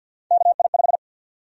Morse code, heard as a keyed tone, about 52 wpm, 700 Hz. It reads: KI5
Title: USA Call Sign Prefixes - 50wpm